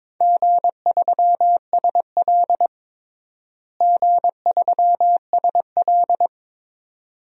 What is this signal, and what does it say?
Morse code, heard as a keyed tone, about 22 wpm, 700 Hz. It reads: G3SL G3SL
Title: DXCC-100 Call Signs - 4 Characters - Repeated 2x - 22wpm